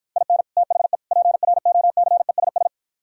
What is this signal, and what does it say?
Morse code, heard as a keyed tone, about 52 wpm, 700 Hz. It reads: IN THE PROCESS